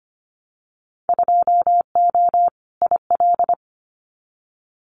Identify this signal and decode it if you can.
Morse code, heard as a keyed tone, about 25 wpm, 700 Hz. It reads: 2O IL
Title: Winter Field Day Exchanges 25wpm